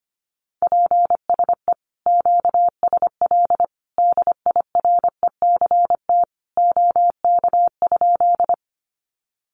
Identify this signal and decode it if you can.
Morse code, heard as a keyed tone, about 25 wpm, 700 Hz. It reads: PSE QSL DIRECT OK?
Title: Common QSO Phrases 25wpm